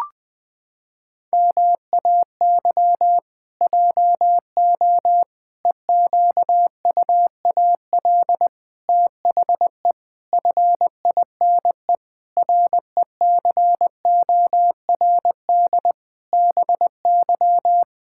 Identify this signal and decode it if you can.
Morse code, heard as a keyed tone, about 20 wpm, 700 Hz. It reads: MAY JO EQUAL THE FINE RECORD BY
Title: Pangram Challenge v2 - 20wpm